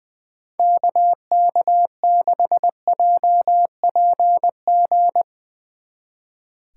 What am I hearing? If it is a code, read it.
Morse code, heard as a keyed tone, about 21 wpm, 700 Hz. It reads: KK6JPG
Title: Speed Racing - USA Call Signs 20wpm